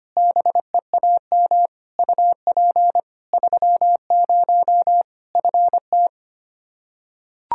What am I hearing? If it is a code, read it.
Morse code, heard as a keyed tone, about 25 wpm, 700 Hz. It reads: BEAM UP 30 FT
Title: Practice QSO - Exchange 2 - Info - 25wpm